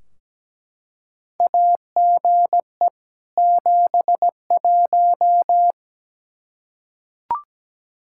Morse code, heard as a keyed tone, about 17 wpm, 700 Hz. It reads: AGE 71
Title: Practice QSO - Exchange 3 - More Info - 17wpm